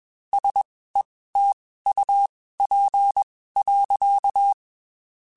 Morse code, heard as a keyed tone, about 21 wpm, spaced as for 15 wpm, 800 Hz. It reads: SETUP.
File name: QSOs / QSOs-mp3s / WD0BGP 1@20